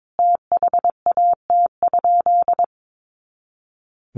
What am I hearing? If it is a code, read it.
Morse code, heard as a keyed tone, about 22 wpm, 700 Hz. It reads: THAT?
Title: Sentences from Top 100 Words 22wpm